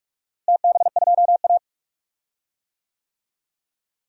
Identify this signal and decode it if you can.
Morse code, heard as a keyed tone, about 46 wpm, 700 Hz. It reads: TB2A
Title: DXCC-100 Call Signs - 4 Characters - Repeated 3x - 45wpm